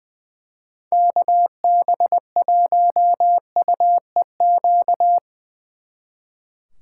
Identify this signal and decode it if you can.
Morse code, heard as a keyed tone, about 20 wpm, 700 Hz. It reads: KB1UEQ
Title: Speed Racing - USA 2x3 Call Signs 20wpm